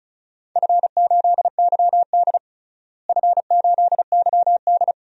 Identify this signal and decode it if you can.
Morse code, heard as a keyed tone, about 36 wpm, 690 Hz. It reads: F8YD F8YD
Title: DXCC-100 Call Signs - 4 Characters - Repeated 2x - 35wpm